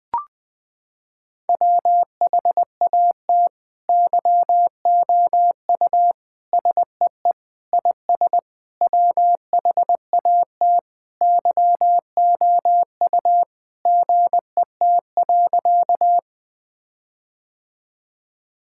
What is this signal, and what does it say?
Morse code, heard as a keyed tone, about 20 wpm, 700 Hz. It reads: WHAT YOU SEE IS WHAT YOU GET.
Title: Sentences with 7-Word Phrases - 20wpm